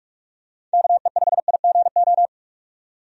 Morse code, heard as a keyed tone, about 44 wpm, 700 Hz. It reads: KE5IGO
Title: USA Call Signs - 6 Characters - 45wpm